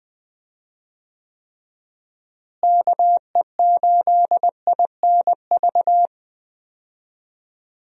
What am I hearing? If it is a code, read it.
Morse code, heard as a keyed tone, about 20 wpm, 700 Hz. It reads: KE8INV
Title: USA Call Signs - 6 Characters - Repeated 3x - 20wpm